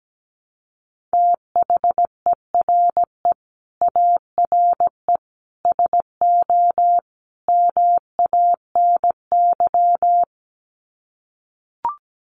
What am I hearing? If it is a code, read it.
Morse code, heard as a keyed tone, about 17 wpm, 700 Hz. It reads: THERE ARE SO MANY
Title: Sets of 4 Words 17wpm